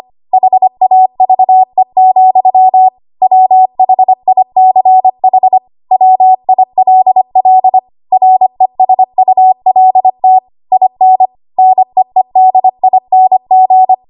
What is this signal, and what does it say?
Morse code, heard as a keyed tone, about 25 wpm, 750 Hz. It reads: HAVE, WHICH WILL RESULT IN NEEDING
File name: Morse / 25WPM